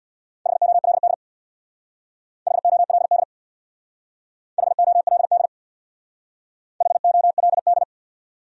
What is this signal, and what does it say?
Morse code, heard as a keyed tone, about 64 wpm, 700 Hz. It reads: HOLD HOLD HOLD HOLD